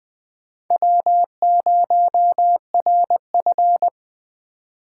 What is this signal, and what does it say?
Morse code, heard as a keyed tone, about 19 wpm, 700 Hz. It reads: W0RF